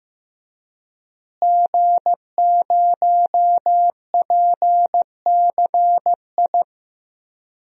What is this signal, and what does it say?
Morse code, heard as a keyed tone, about 15 wpm, 700 Hz. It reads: G0PCI
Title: DXCC-100 Call Signs - 5 Characters - Repeated 3x - 15wpm